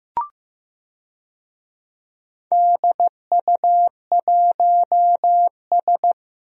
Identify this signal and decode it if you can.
Morse code, heard as a keyed tone, about 15 wpm, 700 Hz. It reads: DU1S
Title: DXCC-100 Call Signs - 4 Characters - 15wpm